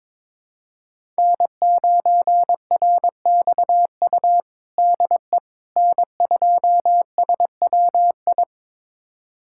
Morse code, heard as a keyed tone, about 22 wpm, 700 Hz. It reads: N9RXU DE N2SWI
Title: USA Callsign DE Callsign - 5 Characters - 22wpm